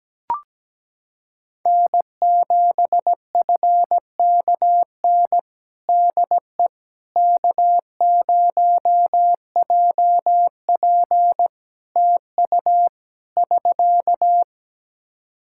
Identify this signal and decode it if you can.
Morse code, heard as a keyed tone, about 17 wpm, 700 Hz. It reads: N7FKN DE K0JP TU <SK>
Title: Practice QSO - Exchange 4 - Ending - 17wpm